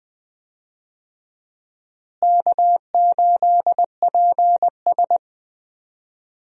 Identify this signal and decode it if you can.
Morse code, heard as a keyed tone, about 20 wpm, 700 Hz. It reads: K8PS
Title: USA Call Signs - 4 Characters - Repeated 3x - 20wpm